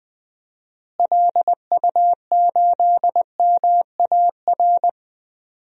Morse code, heard as a keyed tone, about 20 wpm, 700 Hz. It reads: LU8MAR
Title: DXCC-100 Call Signs - 6 Characters - Repeated 2x - 20wpm